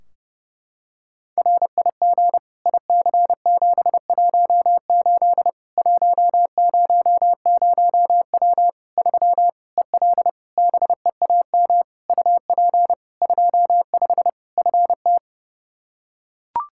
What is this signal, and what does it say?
Morse code, heard as a keyed tone, about 30 wpm, 700 Hz. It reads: RIG IC718 100W 3 EL BEAM UP 25 FT
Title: Practice QSO - Exchange 2 - Info - 30wpm